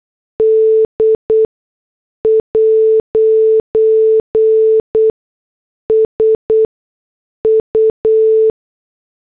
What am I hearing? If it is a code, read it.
Morse code, heard as a keyed tone, about 8 wpm, spaced as for 6 wpm, 440 Hz. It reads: D'S U